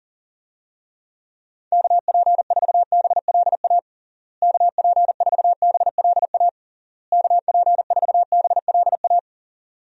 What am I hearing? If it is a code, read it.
Morse code, heard as a keyed tone, about 37 wpm, 700 Hz. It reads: KP4BLA KP4BLA KP4BLA
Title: DXCC-100 Call Signs - 6 Characters - Repeated 3x - 40wpm